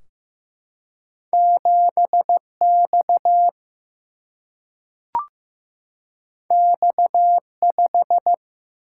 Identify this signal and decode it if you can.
Morse code, heard as a keyed tone, about 15 wpm, 700 Hz. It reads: 7X X5
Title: Single-Letter Single-Number 15wpm